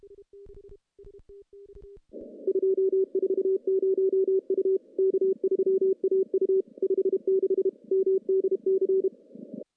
Morse code, heard as a keyed tone, about 28 wpm, 380 Hz. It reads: 240 U K3AU 56 MDC
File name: contests / CW-SS-k5tr- -0000